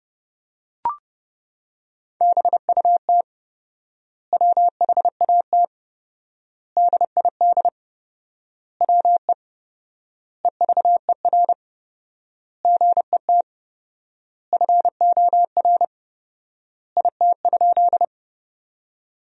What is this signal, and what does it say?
Morse code, heard as a keyed tone, about 32 wpm, 700 Hz. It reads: BUT WHAT DID WE EVER GET FOR IT?
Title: 4x Word-Spacing - Sentences from Top 400 Words - 30wpm